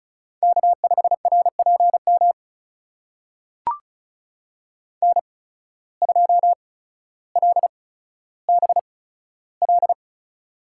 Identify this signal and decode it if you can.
Morse code, heard as a keyed tone, about 36 wpm, spaced as for 15 wpm, 700 Hz. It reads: K5RPM N2LBL
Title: USA Call Signs - 5 Characters - 8x Character Spacing - 35wpm